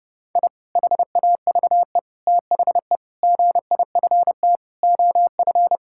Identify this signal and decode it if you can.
Morse code, heard as a keyed tone, about 30 wpm, 700 Hz. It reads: I HAVE THE GIFT OF
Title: Sentences from Top 1000 Words - Review Part 6 - 30wpm